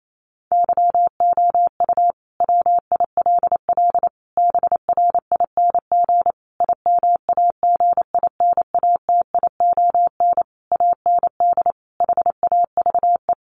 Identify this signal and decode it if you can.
Morse code, heard as a keyed tone, about 28 wpm, 700 Hz. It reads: YOU WILL BRING IMAGINATION AND HAVE
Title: Sentences from Top 1300 Words 28wpm